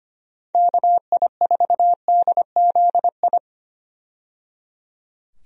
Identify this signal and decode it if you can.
Morse code, heard as a keyed tone, about 25 wpm, 700 Hz. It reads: KI4DZI
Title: USA Call Signs - 6 Characters - 25wpm